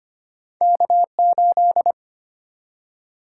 Morse code, heard as a keyed tone, about 25 wpm, 700 Hz. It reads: K8
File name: Two Letter-Number Combinations - 25wpm